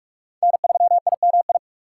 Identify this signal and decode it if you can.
Morse code, heard as a keyed tone, about 46 wpm, 700 Hz. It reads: N3IMI
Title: USA Call Signs - 5 Characters - 2x Character Spacing - 45wpm